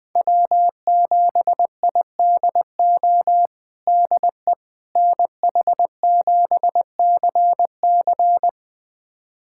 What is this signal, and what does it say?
Morse code, heard as a keyed tone, about 21 wpm, 700 Hz. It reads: W7IDO DE NH7CC
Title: USA Callsign DE Callsign - 5 Characters - 20wpm